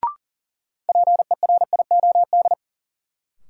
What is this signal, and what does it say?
Morse code, heard as a keyed tone, about 40 wpm, 700 Hz. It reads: PERIOD
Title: Sets of 5 Words - New Word Drill - 40wpm